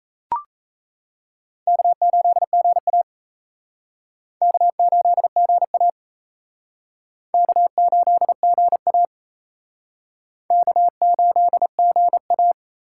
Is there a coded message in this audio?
Yes: Morse code from 1.7 to 12.5 s, heard as a keyed tone, about 35 wpm, 700 Hz. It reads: K8GA K8GA K8GA K8GA